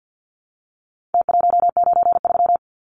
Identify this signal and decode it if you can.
Morse code, heard as a keyed tone, about 48 wpm, 700 Hz. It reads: T193
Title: Numbers - Contest Style - 4-Digit 50wpm